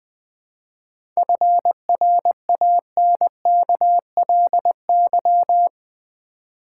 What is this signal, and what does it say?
Morse code, heard as a keyed tone, about 20 wpm, 700 Hz. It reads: FRANKLY
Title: 7 Letter Words - First 3 Letters Given - 20wpm